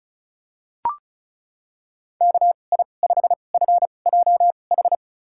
Morse code, heard as a keyed tone, about 34 wpm, spaced as for 25 wpm, 700 Hz. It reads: KI5FJH
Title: USA Call Signs - 6 Characters - 2x Character Spacing - 35wpm